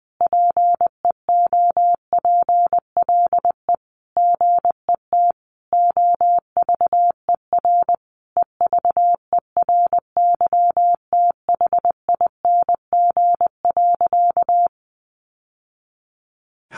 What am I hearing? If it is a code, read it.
Morse code, heard as a keyed tone, about 20 wpm, 700 Hz. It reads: PEOPLE GET OVER EVERYTHING.
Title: Sentences from Top 900 Words 20wpm